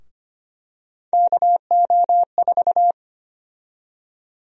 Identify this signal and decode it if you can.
Morse code, heard as a keyed tone, about 25 wpm, 700 Hz. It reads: KO4
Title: USA Call Sign Prefixes - Repeated 2x - 25wpm